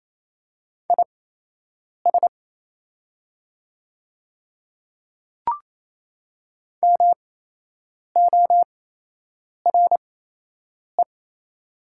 Morse code, heard as a keyed tone, about 28 wpm, spaced as for 8 wpm, 700 Hz. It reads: IS MORE
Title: 8x Character Spacing - Top 100 Words - 28wpm